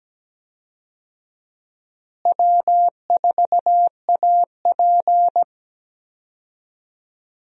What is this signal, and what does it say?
Morse code, heard as a keyed tone, about 17 wpm, 700 Hz. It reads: W4AP
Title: USA Call Signs - 4 Characters - Repeated 3x - 17wpm